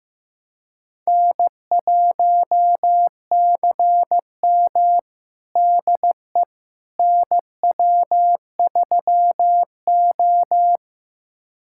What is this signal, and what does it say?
Morse code, heard as a keyed tone, about 16 wpm, 700 Hz. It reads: N1CM DE NW3O
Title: USA Callsign DE Callsign - 4 Characters - 15wpm